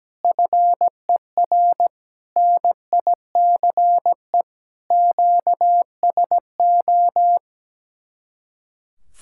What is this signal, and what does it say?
Morse code, heard as a keyed tone, about 17 wpm, 700 Hz. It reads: FER NICE QSO
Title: Speed-Racing Common QSO Phrases - 17wpm